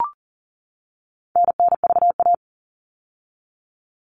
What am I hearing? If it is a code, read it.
Morse code, heard as a keyed tone, about 40 wpm, 700 Hz. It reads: NNVA
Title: All Cut Numbers - 4-Digit 40wpm